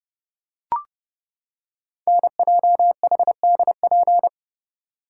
Morse code, heard as a keyed tone, about 30 wpm, 700 Hz. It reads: NJHDP
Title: Code Groups - 5 Letters - 30wpm